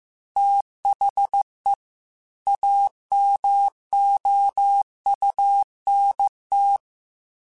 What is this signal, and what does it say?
Morse code, heard as a keyed tone, about 15 wpm, 800 Hz. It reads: THE AMOUNT
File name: Technical Subjects / Technical Subjects-mp3s / XL @13